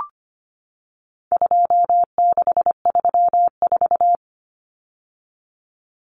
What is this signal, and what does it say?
Morse code, heard as a keyed tone, about 25 wpm, 700 Hz. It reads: 2634
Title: Numbers - 4-Digit 25wpm